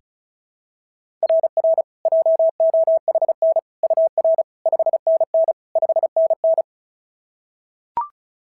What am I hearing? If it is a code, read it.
Morse code, heard as a keyed tone, about 36 wpm, 690 Hz. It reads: RR JOHN UR 5NN 5NN
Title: SOTA QSO Practice 35wpm